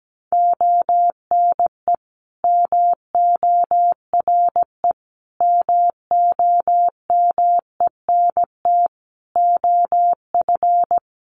Morse code, heard as a keyed tone, about 17 wpm, 700 Hz. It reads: ONE MORE MOMENT OF
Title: Sentences from Top 1300 Words 17wpm